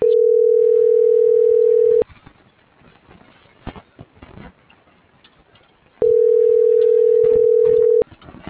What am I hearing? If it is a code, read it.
Ringback tone (North America), 440 + 480 Hz.